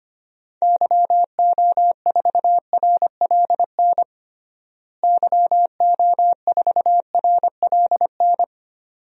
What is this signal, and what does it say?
Morse code, heard as a keyed tone, about 25 wpm, 700 Hz. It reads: YO4RLN YO4RLN
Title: DXCC-100 Call Signs - 6 Characters - Repeated 2x - 25wpm